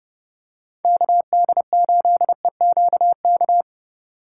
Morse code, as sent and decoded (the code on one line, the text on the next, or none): -.- -.. ---.. --.- -.-
KD8EQK